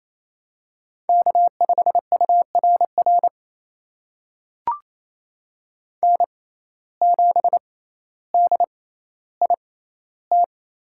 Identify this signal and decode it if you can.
Morse code, heard as a keyed tone, about 28 wpm, spaced as for 14 wpm, 700 Hz. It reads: K5URR N7DIT
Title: USA Call Signs - 5 Characters - 6x Character Spacing - 28wpm